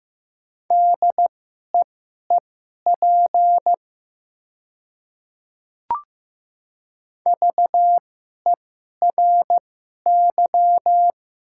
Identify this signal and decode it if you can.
Morse code, heard as a keyed tone, about 16 wpm, spaced as for 11 wpm, 700 Hz. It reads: DEEP VERY
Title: 2x Character Spacing - Top 500 Words - 15wpm